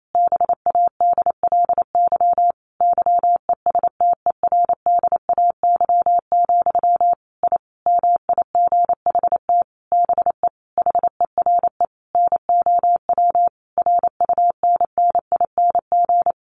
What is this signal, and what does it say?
Morse code, heard as a keyed tone, about 28 wpm, 700 Hz. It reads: BADLY YESTERDAY, I MIGHT BE HERE NOW RUNNING
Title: A Sound Of Thunder 28wpm